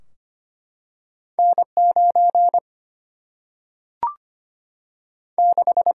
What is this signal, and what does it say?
Morse code, heard as a keyed tone, about 25 wpm, 700 Hz. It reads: N 9 6
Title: Single-Letter Single-Number 25wpm